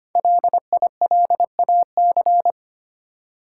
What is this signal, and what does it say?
Morse code, heard as a keyed tone, about 25 wpm, 700 Hz. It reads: LILAC
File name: practice / 25wpm / Easy Words with Hints - 25wpm